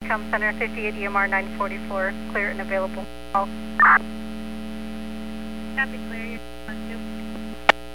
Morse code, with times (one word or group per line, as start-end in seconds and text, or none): none